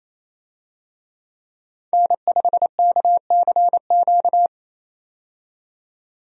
Morse code, heard as a keyed tone, about 28 wpm, 700 Hz. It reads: N5KCQ
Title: USA Call Signs - 5 Characters - 28wpm